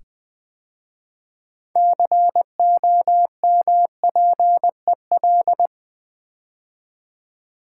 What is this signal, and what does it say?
Morse code, heard as a keyed tone, about 20 wpm, 700 Hz. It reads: COMPEL
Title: 6 Letter Words - First 3 Letters Given - 20wpm